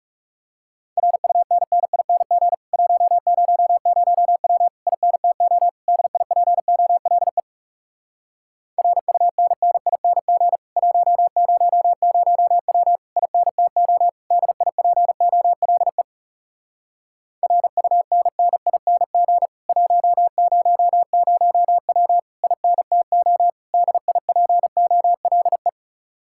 Morse code, heard as a keyed tone, about 40 wpm, 700 Hz. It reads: RUNNING 100W INTO DIPOLE RUNNING 100W INTO DIPOLE RUNNING 100W INTO DIPOLE